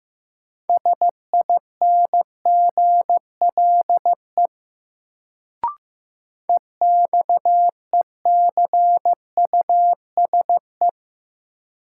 Morse code, heard as a keyed tone, about 15 wpm, 700 Hz. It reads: SINGLE EXECUSE
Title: Sentences from Top 700 Words - New Word Drill - 15wpm